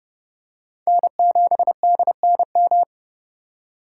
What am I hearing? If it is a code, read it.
Morse code, heard as a keyed tone, about 30 wpm, 700 Hz. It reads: N7DNM
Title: USA Call Signs - 5 Characters - 2x Character Spacing - 30wpm